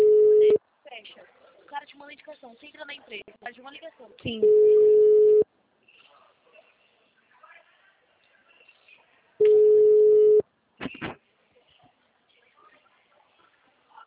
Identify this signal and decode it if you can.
Ringback tone (Europe, 425 Hz), with speech.